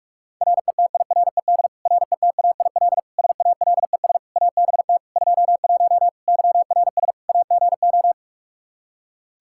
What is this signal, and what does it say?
Morse code, heard as a keyed tone, about 46 wpm, 700 Hz. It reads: RETIRED RETAIL SALES ABT 21 YRS AGO
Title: Practice QSO - Exchange 3 - More Info - 45wpm